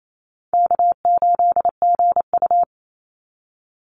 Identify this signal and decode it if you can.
Morse code, heard as a keyed tone, about 28 wpm, 700 Hz. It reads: K8GU
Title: Call Signs - 28wpm